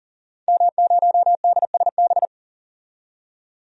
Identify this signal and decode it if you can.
Morse code, heard as a keyed tone, about 40 wpm, 700 Hz. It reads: M0DSB
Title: DXCC-100 Call Signs - 5 Characters - 40wpm